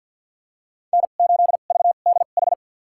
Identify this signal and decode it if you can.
Morse code, heard as a keyed tone, about 50 wpm, spaced as for 36 wpm, 700 Hz. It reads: N8VDH
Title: USA Call Signs - 5 Characters - 2x Character Spacing - 50wpm